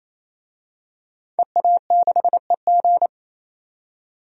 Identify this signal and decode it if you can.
Morse code, heard as a keyed tone, about 28 wpm, 700 Hz. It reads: EA6EG
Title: DXCC-100 Call Signs - 5 Characters - 28wpm